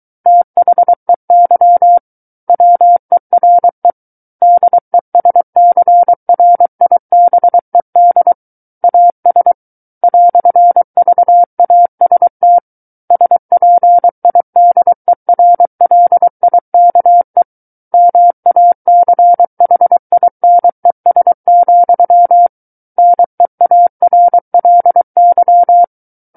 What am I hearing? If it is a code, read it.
Morse code, heard as a keyed tone, about 23 wpm, 700 Hz. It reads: THEY WERE DESCRIBED AS "VAST SPIDERLIKE MACHINES, NEARLY
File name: Book One / 14-Chapter 14 - 23 WPM